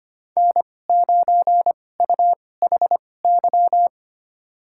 Morse code, heard as a keyed tone, about 25 wpm, spaced as for 18 wpm, 700 Hz. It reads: N9UHY
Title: USA Call Signs - 5 Characters - 2x Character Spacing - 25wpm